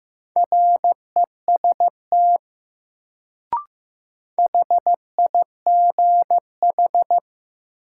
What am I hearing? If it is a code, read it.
Morse code, heard as a keyed tone, about 16 wpm, 700 Hz. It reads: REST HIGH
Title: Sets of 2 Words - New Word Drill - 15wpm